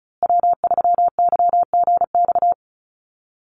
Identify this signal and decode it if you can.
Morse code, heard as a keyed tone, about 34 wpm, 700 Hz. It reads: W3YGX
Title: USA Call Signs Repeated 3x 35wpm